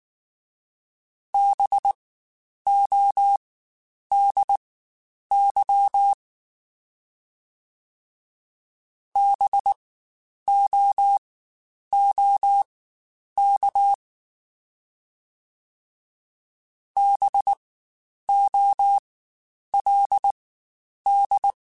Morse code, heard as a keyed tone, about 19 wpm, spaced as for 9 wpm, 800 Hz. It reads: BODY BOOK BOLD
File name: Spaced Alphas / Spaced Alphas mp3s / Bs sp @18